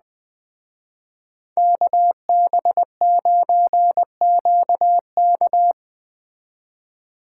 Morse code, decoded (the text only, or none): KB9QK